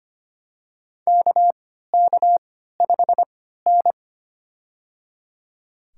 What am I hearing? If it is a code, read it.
Morse code, heard as a keyed tone, about 25 wpm, spaced as for 14 wpm, 700 Hz. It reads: KK5N
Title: USA Call Signs - 4 Characters - 3x Character Spacing - 25wpm